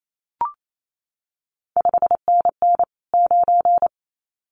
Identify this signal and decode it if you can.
Morse code, heard as a keyed tone, about 28 wpm, 700 Hz. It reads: 5NN 9
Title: CQ WW DX Contest Exchange Encore - 28wpm